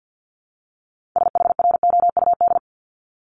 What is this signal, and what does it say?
Morse code, heard as a keyed tone, about 50 wpm, 700 Hz. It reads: SHROUD
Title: Multi-letter Phonemes 50wpm